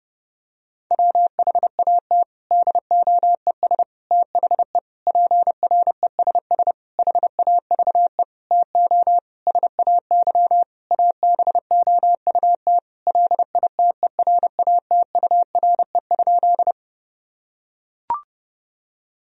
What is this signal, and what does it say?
Morse code, heard as a keyed tone, about 30 wpm, 700 Hz. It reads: WHAT DOES THE PRESS HAVE TO SAY ABOUT LITERATURE?
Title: Sentences from Top 1000 Words - Review Part 3 - 30wpm